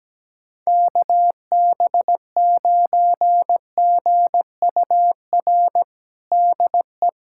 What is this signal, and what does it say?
Morse code, heard as a keyed tone, about 17 wpm, 700 Hz. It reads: KB9GUR DE
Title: USA Callsign DE Callsign 17wpm